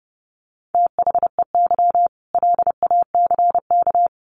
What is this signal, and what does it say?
Morse code, heard as a keyed tone, about 30 wpm, 700 Hz. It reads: THEY LACK
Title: Sentences from Top 1100 Words 30wpm